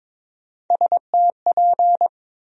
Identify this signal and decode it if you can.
Morse code, heard as a keyed tone, about 22 wpm, 700 Hz. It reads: STP
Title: USA Call Sign Postfixes - 3 Letters 22wpm